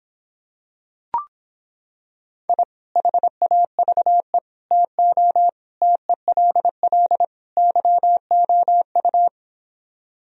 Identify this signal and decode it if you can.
Morse code, heard as a keyed tone, about 26 wpm, 700 Hz. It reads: I HAVE TO TELL YOU